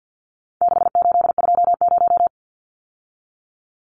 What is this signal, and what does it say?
Morse code, heard as a keyed tone, about 50 wpm, 700 Hz. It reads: 6820
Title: Numbers - 4-Digit 50wpm